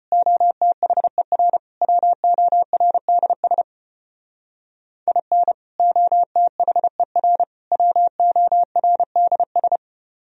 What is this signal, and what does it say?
Morse code, heard as a keyed tone, about 31 wpm, 700 Hz. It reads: OTHER WORDS IN OTHER WORDS
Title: Speed-Racing - Sets of 3 Words 25wpm